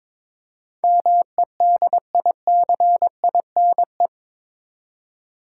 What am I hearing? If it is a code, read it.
Morse code, heard as a keyed tone, about 22 wpm, 700 Hz. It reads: MEDICINE
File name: practice / 22wpm / Eight Letter Words - 22wpm